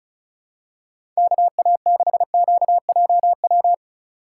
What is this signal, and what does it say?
Morse code, heard as a keyed tone, about 34 wpm, 700 Hz. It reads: KA6QJW
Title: USA Call Signs - 2x Character Spacing - 35wpm